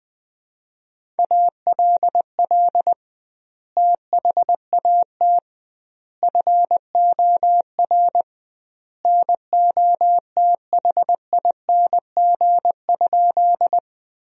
Morse code, heard as a keyed tone, about 21 wpm, 700 Hz. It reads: ALL THAT FOR NOTHING?
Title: 2x Word-Spacing - Sentences from Top 500 Words - 20wpm